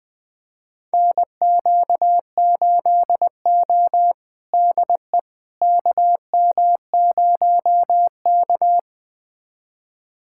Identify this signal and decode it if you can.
Morse code, heard as a keyed tone, about 20 wpm, 700 Hz. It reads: NQ8O DE KM0K
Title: USA Callsign DE Callsign - 4 Characters - 20wpm